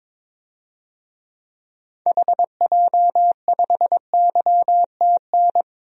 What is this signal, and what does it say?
Morse code, heard as a keyed tone, about 22 wpm, 700 Hz. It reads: HJ5YTN
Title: DXCC-100 Call Signs - 6 Characters - 22wpm